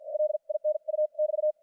Morse code, heard as a keyed tone, about 54 wpm, spaced as for 42 wpm, 610 Hz, with radio noise. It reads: LINU<BT>
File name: archived / CRSnet